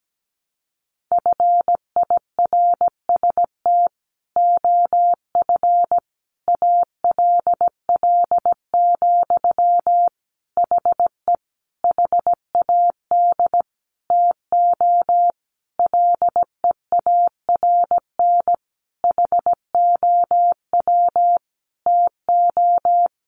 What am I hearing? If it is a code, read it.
Morse code, heard as a keyed tone, about 17 wpm, 700 Hz. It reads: FIRST OF ALL, HE HAD TO LEARN HOW TO
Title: Sentences from Top 1200 Words 17wpm